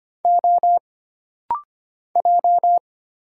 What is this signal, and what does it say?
Morse code, heard as a keyed tone, about 25 wpm, 700 Hz. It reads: O J
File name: Single Character (Letter:Number:Punctuation:Prosign) - Rapid-Fire - 25wpm